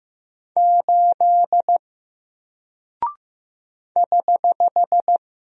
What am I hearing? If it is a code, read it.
Morse code, heard as a keyed tone, about 15 wpm, 700 Hz. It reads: 8 <HH>
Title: Single Character (Letter:Number:Punctuation:Prosign) 15wpm